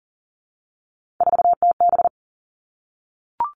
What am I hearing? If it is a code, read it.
Morse code, heard as a keyed tone, about 40 wpm, 700 Hz. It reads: 4TB
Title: All Cut Numbers - 3-Digit 40wpm